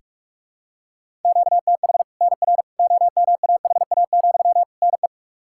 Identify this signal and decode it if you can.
Morse code, heard as a keyed tone, about 41 wpm, 700 Hz. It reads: QTH NR OMAHA, NE
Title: Practice QSO - Exchange 1 - Beginning - 45wpm